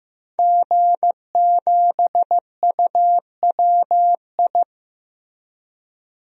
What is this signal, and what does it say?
Morse code, heard as a keyed tone, about 15 wpm, 700 Hz. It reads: G7UWI
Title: DXCC-100 Call Signs - 5 Characters - Repeated 3x - 15wpm